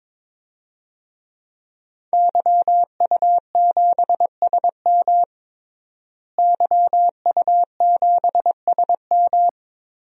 Morse code, heard as a keyed tone, about 21 wpm, 700 Hz. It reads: YU7SM YU7SM
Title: DXCC-100 Call Signs - 5 Characters - Repeated 3x - 22wpm